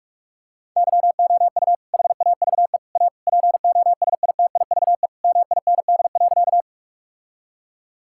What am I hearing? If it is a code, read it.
Morse code, heard as a keyed tone, about 41 wpm, 700 Hz. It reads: YOU HAVE A POSITIVE MIND.
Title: Sentences from Top 1000 Words - Review Part 5 - 45wpm